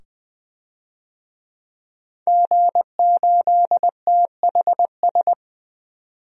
Morse code, heard as a keyed tone, about 20 wpm, 700 Hz. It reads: G8THS
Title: DXCC-100 Call Signs - 5 Characters - 20wpm